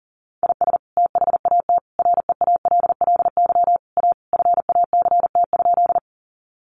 Morse code, heard as a keyed tone, about 40 wpm, 700 Hz. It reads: IS THAT REALLY A FACT?
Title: Sentences from Top 600 Words 40wpm